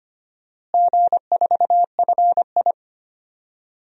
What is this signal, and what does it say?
Morse code, heard as a keyed tone, about 25 wpm, 700 Hz. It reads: G4FI
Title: DXCC-100 Call Signs - 4 Characters - Repeated 3x - 25wpm